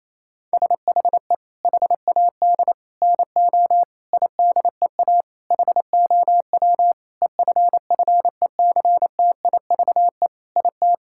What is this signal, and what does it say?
Morse code, heard as a keyed tone, about 28 wpm, 700 Hz. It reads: SHE HAD NO IDEA HOW EFFECTIVE IT
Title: Sentences from Top 1000 Words - Review Part 6 - 28wpm